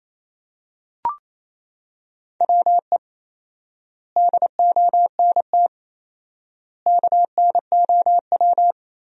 Morse code, heard as a keyed tone, about 28 wpm, 700 Hz. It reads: WE DONT KNOW
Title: 4x Word-Spacing - Sentences from Top 400 Words - 28wpm